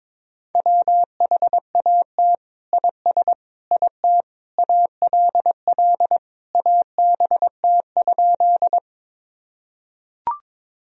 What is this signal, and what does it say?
Morse code, heard as a keyed tone, about 22 wpm, 700 Hz. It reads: WHAT IS IT ALL ABT?
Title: Sentences from Top 100 Words and Abbreviations 22wpm